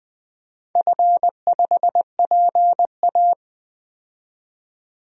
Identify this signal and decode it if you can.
Morse code, heard as a keyed tone, about 20 wpm, 700 Hz. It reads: F5PA
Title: DXCC-100 Call Signs - 4 Characters - 20wpm